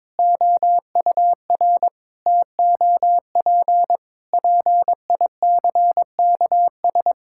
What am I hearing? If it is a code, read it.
Morse code, heard as a keyed tone, about 22 wpm, 700 Hz. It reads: OUR TOP PICKS
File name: Sets of 9 Words - 22wpm